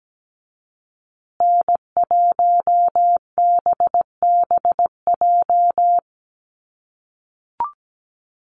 Morse code, heard as a keyed tone, about 17 wpm, 700 Hz. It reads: N1BBJ
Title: USA Call Signs Repeated 3x 17wpm